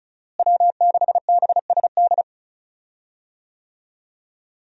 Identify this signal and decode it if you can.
Morse code, heard as a keyed tone, about 36 wpm, 700 Hz. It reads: W6BSD
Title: USA Call Signs - 5 Characters - Repeated 3x - 35wpm